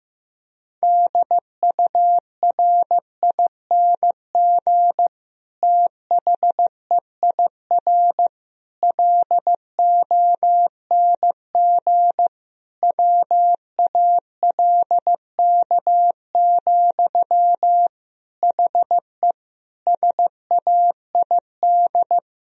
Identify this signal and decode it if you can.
Morse code, heard as a keyed tone, about 15 wpm, 700 Hz. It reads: DURING THEIR LONG WALK, HE SAID
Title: Sentences from Top 1000 Words - Review Part 2 - 15wpm